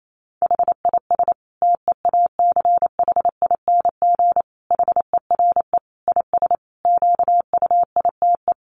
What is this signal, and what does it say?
Morse code, heard as a keyed tone, about 28 wpm, 700 Hz. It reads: HIS TEACHING HERE IS QUITE
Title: Sentences from Top 1100 Words 28wpm